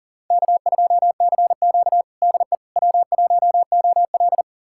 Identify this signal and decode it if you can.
Morse code, heard as a keyed tone, about 40 wpm, 700 Hz. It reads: K2CQ DE W1OL
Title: USA Callsign DE Callsign - 4 Characters - 40wpm